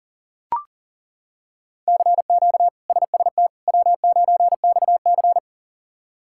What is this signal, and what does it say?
Morse code, heard as a keyed tone, about 40 wpm, 700 Hz. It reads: CQ SST W9XC
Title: K1USN SST QSO Practice - 40wpm